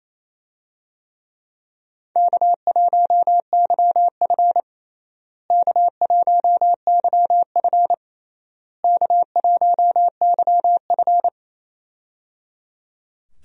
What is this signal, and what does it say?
Morse code, heard as a keyed tone, about 28 wpm, 700 Hz. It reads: K1YF K1YF K1YF
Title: USA Call Signs - 4 Characters - Repeated 3x - 28wpm